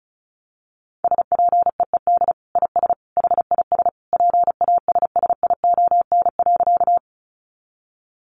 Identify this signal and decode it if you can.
Morse code, heard as a keyed tone, about 34 wpm, 700 Hz. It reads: SPEED IS HIS PASSION.
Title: Sentences from Top 1300 Words 35wpm